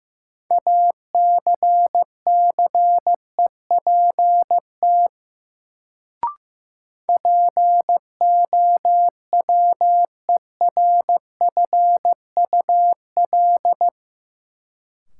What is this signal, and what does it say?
Morse code, heard as a keyed tone, about 15 wpm, 700 Hz. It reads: ACCEPT POWERFUL
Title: Sentences from Top 1300 Words - New Word Drill Alt - 15wpm